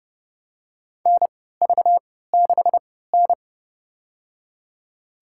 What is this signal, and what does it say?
Morse code, heard as a keyed tone, about 30 wpm, spaced as for 17 wpm, 700 Hz. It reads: NV6N
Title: USA Call Signs - 3x Character Spacing - 30wpm